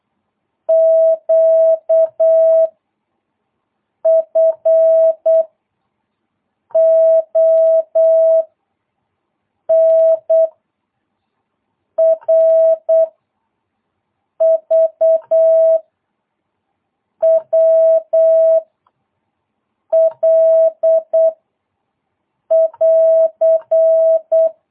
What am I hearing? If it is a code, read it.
Morse code, heard as a keyed tone, about 8 wpm, 660 Hz. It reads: Q F O N R V W L <AR>
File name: jh8cbh / 2022cw